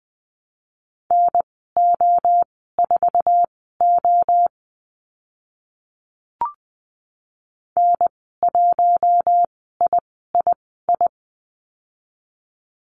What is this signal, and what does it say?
Morse code, heard as a keyed tone, about 20 wpm, spaced as for 15 wpm, 700 Hz. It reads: NO4O N1III
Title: 069 Lesson 12g - Letter d - Call Signs 20wpm